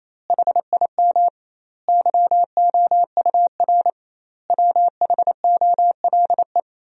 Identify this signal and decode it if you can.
Morse code, heard as a keyed tone, about 28 wpm, 700 Hz. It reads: HIM YOUR WHOLE
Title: 2x Word-Spacing - Sentences from Top 400 Words - 28wpm